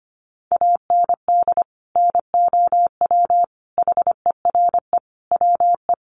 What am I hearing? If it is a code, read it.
Morse code, heard as a keyed tone, about 25 wpm, 700 Hz. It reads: AND NOW HERE WE
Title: Sentences from Top 1400 Words 25wpm